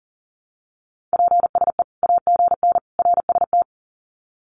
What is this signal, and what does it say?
Morse code, heard as a keyed tone, about 40 wpm, 700 Hz. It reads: PSE AGN RST
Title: Common QSO Phrases 40wpm